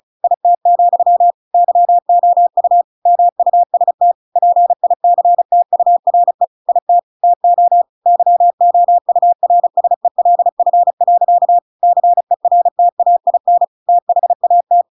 Morse code, heard as a keyed tone, about 35 wpm, 700 Hz. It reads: IT, YOU MUST PICTURE IT TO YOURSELF. CERTAIN THAT